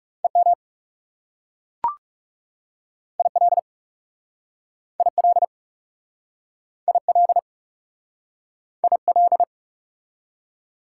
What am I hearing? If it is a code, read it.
Morse code, heard as a keyed tone, about 40 wpm, 690 Hz. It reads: EM IL IL IL IL